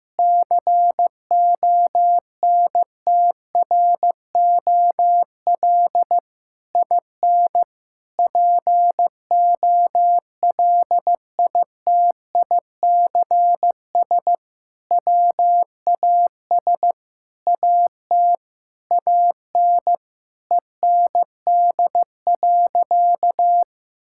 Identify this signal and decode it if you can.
Morse code, heard as a keyed tone, about 15 wpm, 700 Hz. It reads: CONTROL IN POLITICS WAS AT AN END.
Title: Sentences from Top 1000 Words - Review Part 6 - 15wpm